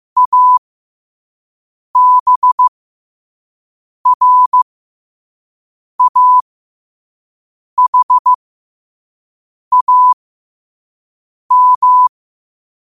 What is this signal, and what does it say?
Morse code, heard as a keyed tone, about 15 wpm, spaced as for 5 wpm, 1000 Hz. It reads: ABRAHAM
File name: qotdfiles / 5wpm